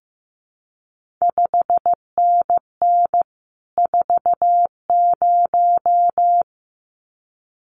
Morse code, heard as a keyed tone, about 15 wpm, 700 Hz. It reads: 5NN 40
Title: CQ WW DX Contest Exchange Encore - 15wpm